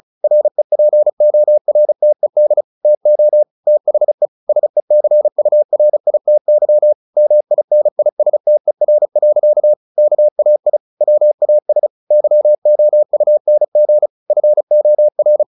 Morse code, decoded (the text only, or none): REPORTED TO THE SECURITY MINISTER. KAI WAS YOUNG FOR